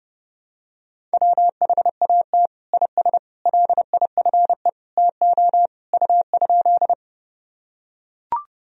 Morse code, heard as a keyed tone, about 32 wpm, 700 Hz. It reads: WHAT IS LIFE TO U?
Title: Sentences from Top 300 Words and Abbreviations 30wpm